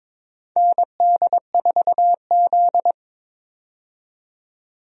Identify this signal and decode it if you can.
Morse code, heard as a keyed tone, about 21 wpm, 700 Hz. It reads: ND4Z
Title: USA Call Signs - 4 Characters - 22wpm